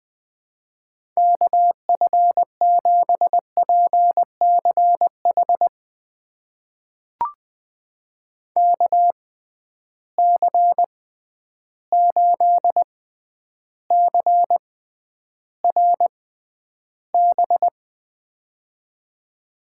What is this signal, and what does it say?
Morse code, heard as a keyed tone, about 20 wpm, spaced as for 10 wpm, 700 Hz. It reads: KF7PCH KC8CRB
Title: USA Call Signs - 6 Characters - 6x Character Spacing - 20wpm